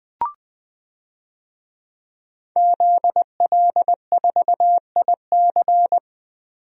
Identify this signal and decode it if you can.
Morse code, heard as a keyed tone, about 20 wpm, 700 Hz. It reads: ZL4IC
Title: DXCC-100 Call Signs - 5 Characters - 20wpm